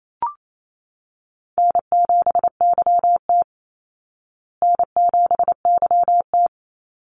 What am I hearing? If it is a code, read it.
Morse code, heard as a keyed tone, about 28 wpm, 700 Hz. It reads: N7YT N7YT
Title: Call Signs Repeated 2x - 28wpm